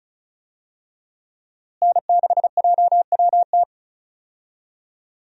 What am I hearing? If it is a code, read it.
Morse code, heard as a keyed tone, about 36 wpm, 700 Hz. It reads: N6JWT
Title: USA Call Signs - 5 Characters - Repeated 2x - 35wpm